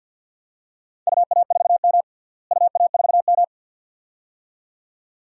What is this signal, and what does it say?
Morse code, heard as a keyed tone, about 48 wpm, 700 Hz. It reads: UA4M UA4M
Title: DXCC-100 Call Signs - 4 Characters - Repeated 2x - 50wpm